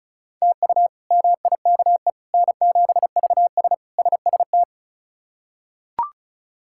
Morse code, heard as a keyed tone, about 36 wpm, 700 Hz. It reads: TU MIKE N7VS SST
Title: K1USN SST QSO Practice - 35wpm